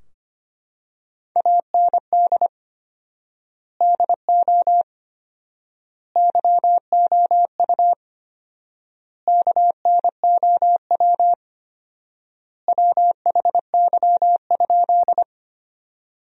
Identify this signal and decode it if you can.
Morse code, heard as a keyed tone, about 25 wpm, 700 Hz. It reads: AND DO YOU KNOW WHY?
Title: 4x Word-Spacing - Sentences from Top 200 Words - 25wpm